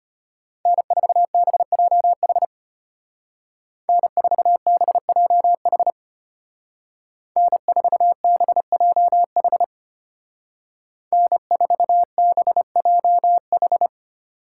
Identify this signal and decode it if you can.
Morse code, heard as a keyed tone, about 33 wpm, 700 Hz. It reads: N4BJH N4BJH N4BJH N4BJH